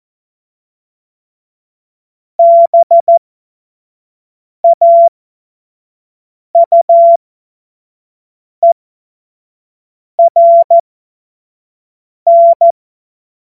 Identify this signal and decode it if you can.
Morse code, heard as a keyed tone, about 14 wpm, spaced as for 5 wpm, 680 Hz. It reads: BAUERN
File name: Morsekurs / Witze / Witz 01. März 2023 25 F70